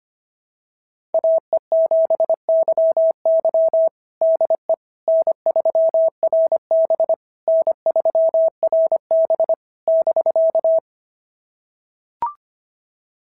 Morse code, heard as a keyed tone, about 25 wpm, 650 Hz. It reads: AE7YY DE N3RB N3RB <BK>
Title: SOTA QSO Practice 25wpm